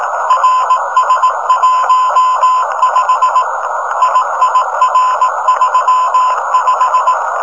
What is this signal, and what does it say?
Morse code, heard as a keyed tone, about 18 wpm, 1000 Hz, with noise. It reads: RS15 IIR35
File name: rs-15 cw telemetry